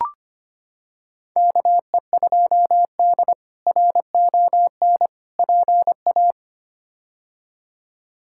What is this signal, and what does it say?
Morse code, heard as a keyed tone, about 25 wpm, 700 Hz. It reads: KE2D RON PA
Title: K1USN SST QSO Practice - 25wpm